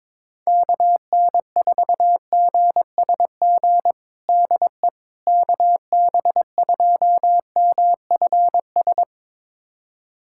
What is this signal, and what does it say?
Morse code, heard as a keyed tone, about 22 wpm, 700 Hz. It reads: KN4GSG DE KB2MFS
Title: USA Callsign DE Callsign - 6 Characters - 22wpm